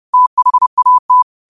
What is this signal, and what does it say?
Morse code, heard as a keyed tone, about 30 wpm, 1000 Hz. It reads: THAT